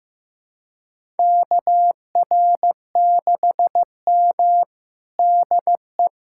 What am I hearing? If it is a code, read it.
Morse code, heard as a keyed tone, about 15 wpm, 700 Hz. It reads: KR6M DE
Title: USA Callsign DE Callsign - 4 Characters - 15wpm